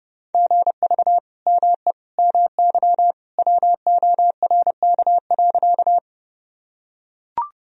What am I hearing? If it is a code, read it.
Morse code, heard as a keyed tone, about 32 wpm, 700 Hz. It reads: GV ME MY WORK.
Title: Sentences from Top 200 Words and Abbreviations 30wpm